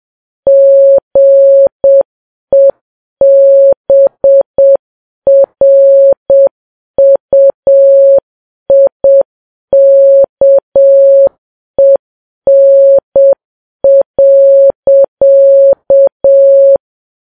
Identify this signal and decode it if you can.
Morse code, heard as a keyed tone, about 7 wpm, 550 Hz. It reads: GEBRUIKEN.
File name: Files Web tls / Morse on9fv / morse7kps